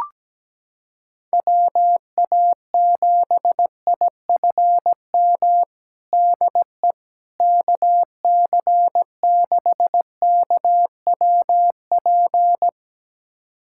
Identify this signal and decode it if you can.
Morse code, heard as a keyed tone, about 17 wpm, 700 Hz. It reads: WA7IFM DE KC6KWP
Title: USA Callsign DE Callsign - 6 Characters - 17wpm